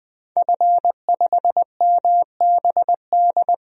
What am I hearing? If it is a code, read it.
Morse code, heard as a keyed tone, about 21 wpm, 700 Hz. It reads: F5MBD
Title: DXCC-100 Call Signs - 5 Characters - 20wpm